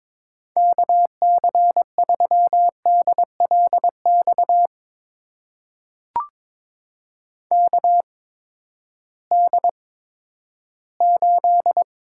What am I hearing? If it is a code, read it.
Morse code, heard as a keyed tone, about 22 wpm, spaced as for 13 wpm, 700 Hz. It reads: KC3DLX KD8
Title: USA Call Signs - 6 Characters - 8x Character Spacing - 22wpm